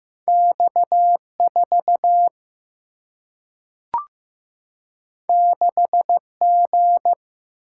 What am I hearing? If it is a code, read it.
Morse code, heard as a keyed tone, about 15 wpm, 700 Hz. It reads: X 4 6 G
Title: Single-Letter Single-Number 15wpm